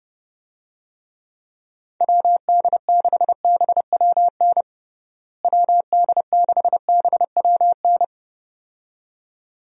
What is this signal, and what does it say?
Morse code, heard as a keyed tone, about 30 wpm, 700 Hz. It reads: WD6BWN WD6BWN
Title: USA Call Signs - 6 Characters - Repeated 2x - 30wpm